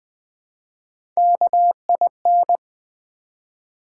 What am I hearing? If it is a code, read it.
Morse code, heard as a keyed tone, about 20 wpm, 700 Hz. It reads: KIN
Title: Build A Word - 1 Letter At A Time - 4-Letter Words - 20wpm